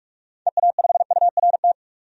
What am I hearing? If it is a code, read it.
Morse code, heard as a keyed tone, about 43 wpm, 700 Hz. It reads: EA5URT
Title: DXCC-100 Call Signs - 6 Characters - 45wpm